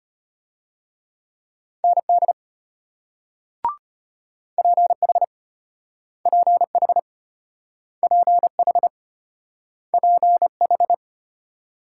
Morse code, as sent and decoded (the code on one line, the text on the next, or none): -. -.. / .--. .... / .--. .... / .--. .... / .--. ....
ND PH PH PH PH